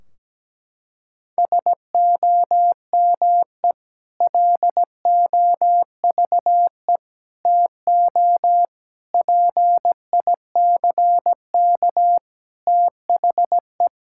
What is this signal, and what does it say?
Morse code, heard as a keyed tone, about 17 wpm, 700 Hz. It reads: SOME LOVE TO PICK THE
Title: Sentences from Top 1600 Words 17wpm